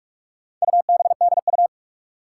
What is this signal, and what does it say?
Morse code, heard as a keyed tone, about 43 wpm, 700 Hz. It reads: UBDU
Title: Four Challenging Letter-Number Combinations 45wpm